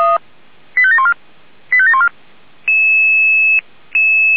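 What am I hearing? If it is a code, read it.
Five-tone selective call (ZVEI), heard as quick runs of five tones, tones of about 70 ms. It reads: 85714 85714
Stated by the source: Five-tone ZVEI selective call received on an SDR.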